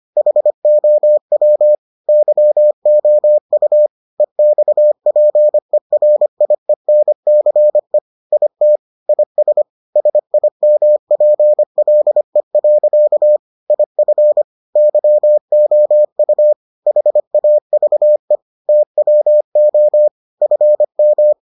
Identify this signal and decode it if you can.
Morse code, heard as a keyed tone, about 25 wpm, 600 Hz. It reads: HOW YOU EXPERIENCE IT IS SIMPLE. IF YOU HAVE TWO FM